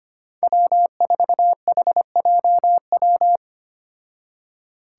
Morse code, heard as a keyed tone, about 25 wpm, 700 Hz. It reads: W4HJW
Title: Five Challenging Letter-Number Combinations 25wpm